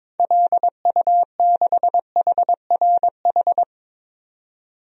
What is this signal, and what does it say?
Morse code, heard as a keyed tone, about 21 wpm, 700 Hz. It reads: LU6HRH
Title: DXCC-100 Call Signs - 6 Characters - 22wpm